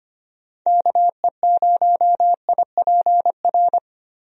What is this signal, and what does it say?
Morse code, heard as a keyed tone, about 25 wpm, 700 Hz. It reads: KE0IPR
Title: USA Call Signs - 6 Characters - Repeated 3x - 25wpm